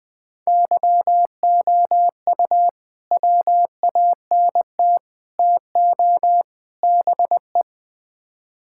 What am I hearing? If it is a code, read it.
Morse code, heard as a keyed tone, about 20 wpm, 700 Hz. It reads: YOU WANT TO BE
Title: Build A Phrase - 4-Word Phrases - 20wpm